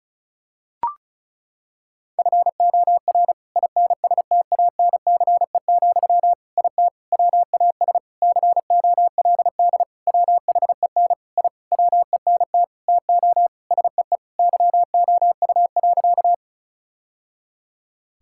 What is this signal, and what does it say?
Morse code, heard as a keyed tone, about 36 wpm, 700 Hz. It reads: FOR INSTANCE, IT WAS COLD WHEN I WENT TO SEE YOU.
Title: Sentences from Top 1000 Words - Review Part 5 - 35wpm